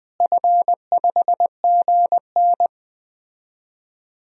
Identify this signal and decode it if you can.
Morse code, heard as a keyed tone, about 20 wpm, 700 Hz. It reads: F5GN
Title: DXCC-100 Call Signs - 4 Characters - Repeated 3x - 20wpm